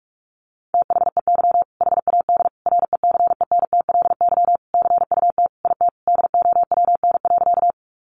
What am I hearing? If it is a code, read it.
Morse code, heard as a keyed tone, about 44 wpm, 700 Hz. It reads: THEY HAD RECENTLY CUT IT DOWN.
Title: Sentences from Top 1200 Words 45wpm